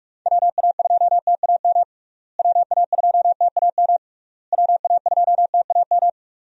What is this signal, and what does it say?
Morse code, heard as a keyed tone, about 46 wpm, 700 Hz. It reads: WA2TAM WA2TAM WA2TAM
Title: USA Call Signs - 6 Characters - Repeated 3x - 45wpm